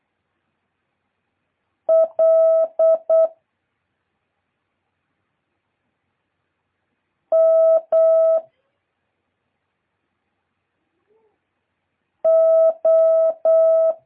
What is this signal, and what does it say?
Morse code, heard as a keyed tone, about 8 wpm, 650 Hz. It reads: L M O
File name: jh8cbh / 2023cw